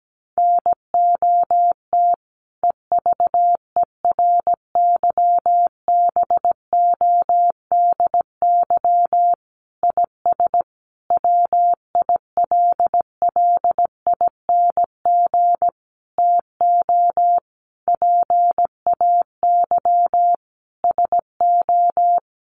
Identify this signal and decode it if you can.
Morse code, heard as a keyed tone, about 17 wpm, 700 Hz. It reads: NOT EVERYBODY IS WILLING TO PAY SO
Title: Sentences from Top 1400 Words 17wpm